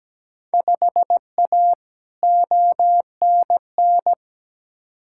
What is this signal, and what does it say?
Morse code, heard as a keyed tone, about 17 wpm, 700 Hz. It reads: 5A ONN
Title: Field Day Exchanges Encore - 17wpm